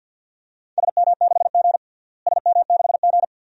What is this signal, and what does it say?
Morse code, heard as a keyed tone, about 51 wpm, 700 Hz. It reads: SM6G SM6G
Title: DXCC-100 Call Signs - 4 Characters - Repeated 2x - 50wpm